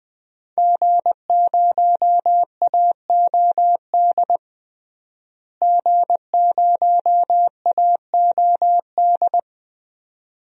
Morse code, heard as a keyed tone, about 20 wpm, 700 Hz. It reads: G0AOD G0AOD
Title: DXCC-100 Call Signs - 5 Characters - Repeated 2x - 20wpm